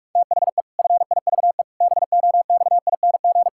Morse code, heard as a keyed tone, about 46 wpm, 700 Hz. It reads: THE FIVE BOXING
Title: Pangram Challenge v2 - 45wpm